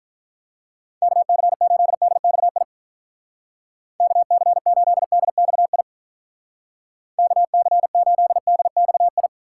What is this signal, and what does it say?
Morse code, heard as a keyed tone, about 46 wpm, 700 Hz. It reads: KC8DXI KC8DXI KC8DXI